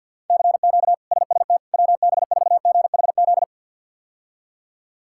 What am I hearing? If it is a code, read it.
Morse code, heard as a keyed tone, about 50 wpm, 700 Hz. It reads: CQ SST WB4GHZ
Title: K1USN SST QSO Practice - 50wpm